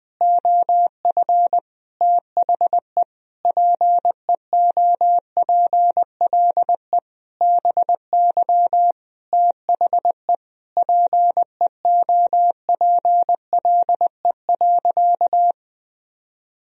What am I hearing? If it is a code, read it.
Morse code, heard as a keyed tone, about 20 wpm, 700 Hz. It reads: OF THE PEOPLE BY THE PEOPLE.
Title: Sentences from Top 1000 Words - Review Part 3 - 20wpm